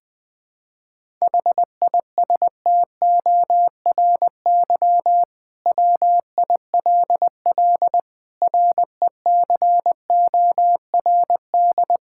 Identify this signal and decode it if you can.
Morse code, heard as a keyed tone, about 20 wpm, 700 Hz. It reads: HISTORY WILL RECORD
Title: Sentences from Top 1600 Words 20wpm